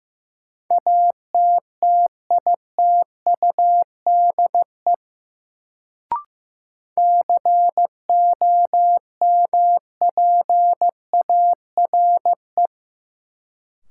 Morse code, heard as a keyed tone, about 15 wpm, 700 Hz. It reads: ATTITUDE COMPARE
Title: Sentences from Top 1300 Words - New Word Drill Alt - 15wpm